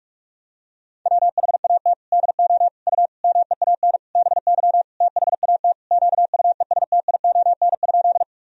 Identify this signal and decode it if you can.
Morse code, heard as a keyed tone, about 41 wpm, 700 Hz. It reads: WHAT DO U MEAN BY THAT QUESTION?
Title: Sentences from Top 400 Words and Abbreviations 45wpm